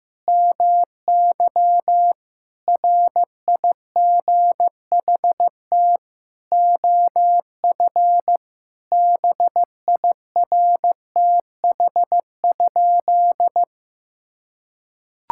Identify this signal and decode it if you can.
Morse code, heard as a keyed tone, about 15 wpm, 700 Hz. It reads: MY RIGHT OF BIRTH?
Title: Sentences from Top 1000 Words - Review Part 6 - 15wpm